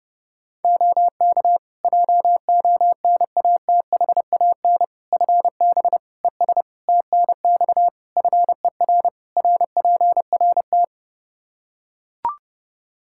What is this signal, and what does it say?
Morse code, heard as a keyed tone, about 30 wpm, 700 Hz. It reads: OK JONATHAN FB ES TNX FER RPRT
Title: Practice QSO - Exchange 2 - Info - 30wpm